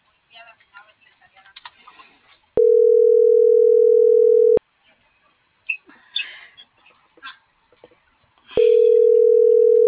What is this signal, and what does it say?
Ringback tone (North America), 440 + 480 Hz.